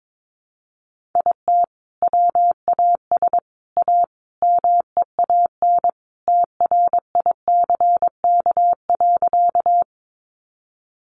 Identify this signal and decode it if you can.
Morse code, heard as a keyed tone, about 22 wpm, 700 Hz. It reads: IT WAS A MEAN TRICK.
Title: Sentences from Top 1500 Words 22wpm